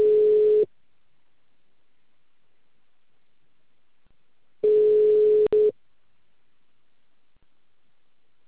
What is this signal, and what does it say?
Ringback tone (Europe, 425 Hz).
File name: out- -2044- - - .19580